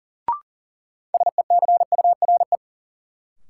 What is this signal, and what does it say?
Morse code, heard as a keyed tone, about 37 wpm, 700 Hz. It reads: SECURE
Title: Sentences from Top 1200 Words - New Word Drill Alt - 40wpm